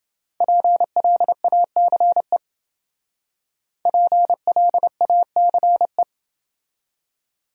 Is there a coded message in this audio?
Yes: Morse code from 0.4 to 6.0 s, heard as a keyed tone, about 29 wpm, 700 Hz. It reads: PLACE PLACE